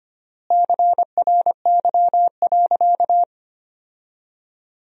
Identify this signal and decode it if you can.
Morse code, heard as a keyed tone, about 25 wpm, 700 Hz. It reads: CRY.
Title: 3x Word-Spacing - Sentences from Top 500 Words - 25wpm